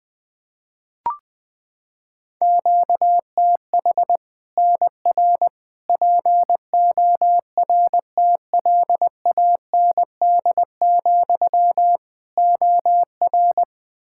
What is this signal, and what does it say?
Morse code, heard as a keyed tone, about 20 wpm, 700 Hz. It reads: QTH NR PORTLAND, OR
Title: Practice QSO - Exchange 1 - Beginning - 20wpm